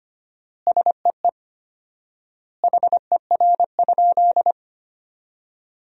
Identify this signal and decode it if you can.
Morse code, heard as a keyed tone, about 25 wpm, 700 Hz. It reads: SEE HER?
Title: 4x Word-Spacing - Sentences from Top 100 Words - 25wpm